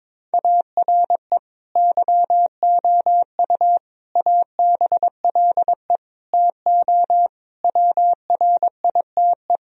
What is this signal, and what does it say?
Morse code, heard as a keyed tone, about 22 wpm, 700 Hz. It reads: ARE YOU ABLE TO WRITE
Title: Sentences from Top 1000 Words - Review Part 3 - 22wpm